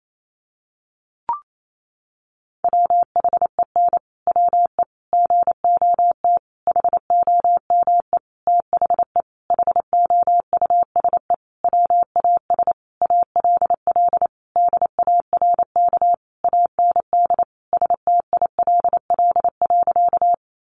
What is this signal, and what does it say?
Morse code, heard as a keyed tone, about 28 wpm, 700 Hz. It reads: WHEN WE GOT HOME THE HOUSE WAS ALL DARK AND STILL.
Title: Sentences from Top 500 Words 28wpm